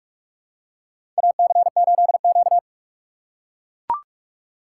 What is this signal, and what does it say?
Morse code, heard as a keyed tone, about 43 wpm, 700 Hz. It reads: AC8Q
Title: USA Call Signs - 4 Characters - 3x Character Spacing - 45wpm